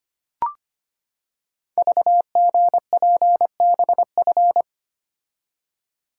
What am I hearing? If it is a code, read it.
Morse code, heard as a keyed tone, about 25 wpm, 700 Hz. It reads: VGPBF
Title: Code Groups - 5 Letters - 25wpm